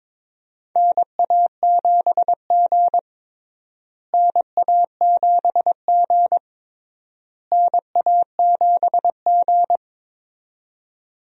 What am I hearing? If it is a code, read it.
Morse code, heard as a keyed tone, about 22 wpm, 700 Hz. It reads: NA7G NA7G NA7G
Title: USA Call Signs - 4 Characters - Repeated 3x - 22wpm